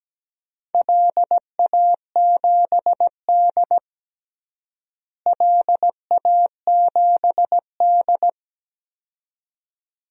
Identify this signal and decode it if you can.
Morse code, heard as a keyed tone, about 17 wpm, 700 Hz. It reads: LA7D LA7D
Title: DXCC-100 Call Signs - 4 Characters - Repeated 2x - 17wpm